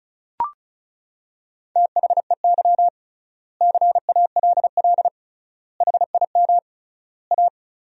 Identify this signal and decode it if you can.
Morse code, heard as a keyed tone, about 34 wpm, 700 Hz. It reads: THEY CALL HIM A
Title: 3x Word-Spacing - Sentences from Top 500 Words - 35wpm